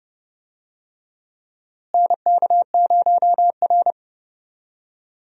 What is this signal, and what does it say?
Morse code, heard as a keyed tone, about 30 wpm, 700 Hz. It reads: NK0R
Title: USA Call Signs - 4 Characters - Repeated 2x - 30wpm